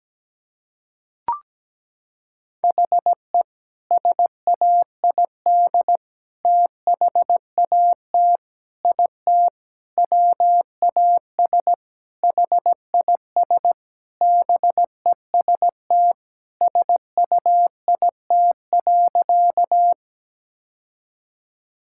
Morse code, heard as a keyed tone, about 17 wpm, 700 Hz. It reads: HE SAID THAT IT WAS HIS BEST SUIT.
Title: Sentences from Top 1600 Words 17wpm